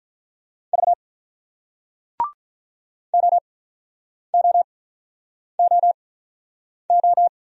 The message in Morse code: V OOOO

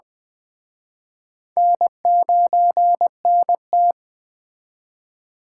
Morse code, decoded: N9NT

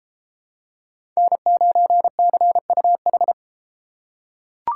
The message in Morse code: N9CUH

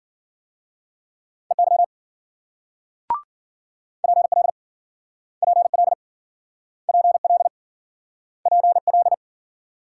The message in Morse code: EY PL PL PL PL

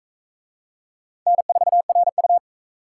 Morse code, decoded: N4RU

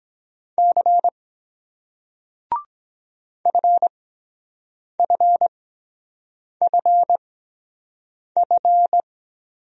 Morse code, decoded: C FFFF

